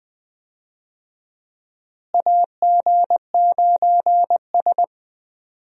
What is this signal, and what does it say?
Morse code, heard as a keyed tone, about 20 wpm, 700 Hz. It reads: AG9S
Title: USA Call Signs - 4 Characters - Repeated 2x - 20wpm